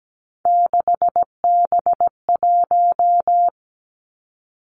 Morse code, heard as a keyed tone, about 17 wpm, 700 Hz. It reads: 6B1
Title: Three Challenging Letter-Number Combinations 17wpm